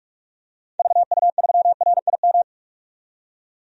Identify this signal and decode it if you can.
Morse code, heard as a keyed tone, about 46 wpm, 700 Hz. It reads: VU3RIM
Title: DXCC-100 Call Signs - 6 Characters - 45wpm